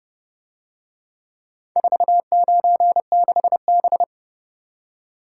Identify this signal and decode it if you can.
Morse code, heard as a keyed tone, about 30 wpm, 700 Hz. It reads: 496B